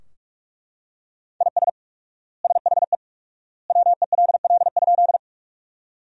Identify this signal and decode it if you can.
Morse code, heard as a keyed tone, about 41 wpm, 700 Hz. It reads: IS SHE WELL?
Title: 4x Word-Spacing - Sentences from Top 200 Words - 45wpm